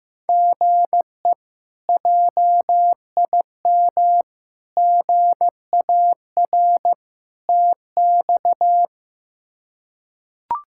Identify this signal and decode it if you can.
Morse code, heard as a keyed tone, about 16 wpm, 700 Hz. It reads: GE JIM GAR TX
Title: K1USN SST QSO Practice - 15wpm